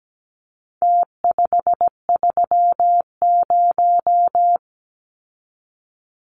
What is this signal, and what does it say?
Morse code, heard as a keyed tone, about 17 wpm, 700 Hz. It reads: T530
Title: Numbers - Contest Style - 4-Digit 17wpm